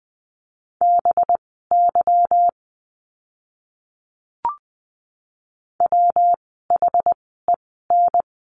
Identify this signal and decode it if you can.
Morse code, heard as a keyed tone, about 20 wpm, spaced as for 15 wpm, 700 Hz. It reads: BY WHEN
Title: 185 Lesson 30f - Letter b - Words 20wpm